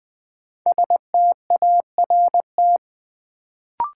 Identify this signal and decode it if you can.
Morse code, heard as a keyed tone, about 20 wpm, 700 Hz. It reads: START
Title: Sentences from Top 300 Words - New Word Drill - 20wpm